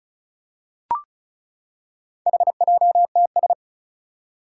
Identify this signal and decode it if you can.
Morse code, heard as a keyed tone, about 36 wpm, 690 Hz. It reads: HJTS
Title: Four Challenging Letter-Number Combinations 35wpm